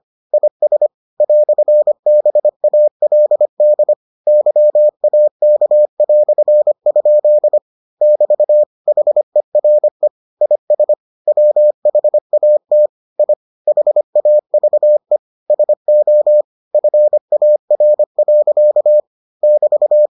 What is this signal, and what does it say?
Morse code, heard as a keyed tone, about 25 wpm, 600 Hz. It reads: IS "BALD YAK"? <BT> HERE IS WHAT I HAVE SO FAR. <BT>